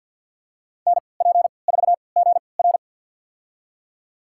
Morse code, heard as a keyed tone, about 52 wpm, spaced as for 28 wpm, 690 Hz. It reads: NP4GR
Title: USA Call Signs - 3x Character Spacing - 50wpm